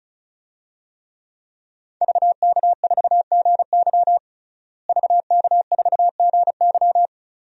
Morse code, heard as a keyed tone, about 34 wpm, 700 Hz. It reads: VK4GY VK4GY
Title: DXCC-100 Call Signs - 5 Characters - Repeated 2x - 35wpm